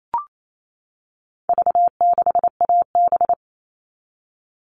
Morse code, heard as a keyed tone, about 28 wpm, 700 Hz. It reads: V6AB
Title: All Cut Numbers - 4-Digit 28wpm